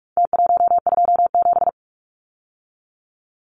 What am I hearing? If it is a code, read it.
Morse code, heard as a keyed tone, about 44 wpm, 700 Hz. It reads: T127
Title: Numbers - Contest Style - 4-Digit 45wpm